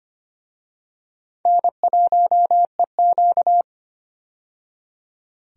Morse code, heard as a keyed tone, about 25 wpm, 700 Hz. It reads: N1EQ
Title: USA Call Signs - 4 Characters - 25wpm